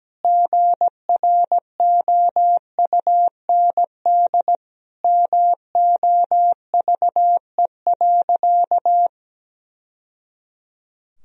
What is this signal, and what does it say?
Morse code, heard as a keyed tone, about 17 wpm, 700 Hz. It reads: GROUND MOVE.
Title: Sentences from Top 1000 Words - Review Part 2 - 17wpm